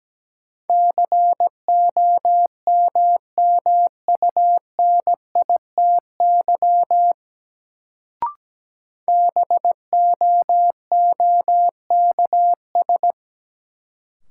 Morse code, heard as a keyed tone, about 17 wpm, 700 Hz. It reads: COMMUNITY BOOKS
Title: Sentences from Top 1000 Words - Review Part 4 - New Word Drill - 17wpm